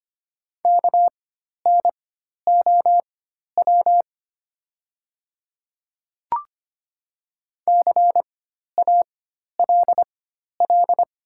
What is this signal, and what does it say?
Morse code, heard as a keyed tone, about 25 wpm, spaced as for 12 wpm, 700 Hz. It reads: KNOW CALL
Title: 4x Character Spacing - Top 100 Words - 25wpm